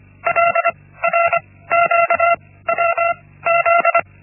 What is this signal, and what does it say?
Morse code, heard as a keyed tone, about 25 wpm, spaced as for 16 wpm, 680 Hz. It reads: LRQWZ
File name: Linksmakalnis-1